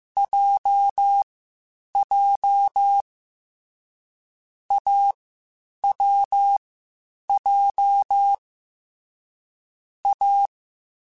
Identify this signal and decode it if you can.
Morse code, heard as a keyed tone, about 15 wpm, 800 Hz. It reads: J J A W J A
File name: Difficult Combos / Difficult Combos-mp3s / AWJ ShortRun @13